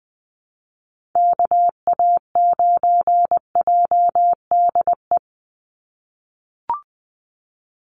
Morse code, heard as a keyed tone, about 20 wpm, 700 Hz. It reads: KA9JDE
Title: USA Call Signs 20wpm